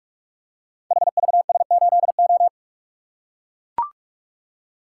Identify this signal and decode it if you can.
Morse code, heard as a keyed tone, about 44 wpm, 690 Hz. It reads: HVS8O
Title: Code Groups - 5 Letter:Numbers - 45wpm